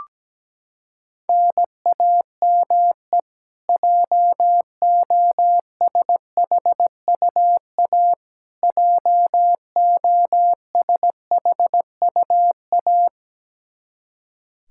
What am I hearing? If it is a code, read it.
Morse code, heard as a keyed tone, about 17 wpm, 700 Hz. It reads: NAME JOSHUA JOSHUA
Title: Practice QSO - Exchange 1 - Beginning - 17wpm